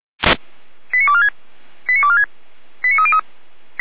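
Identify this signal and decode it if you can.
Five-tone selective call (ZVEI), heard as quick runs of five tones, tones of about 70 ms. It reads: 89246 89246 89303
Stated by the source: ZVEI five-tone call received by SDR.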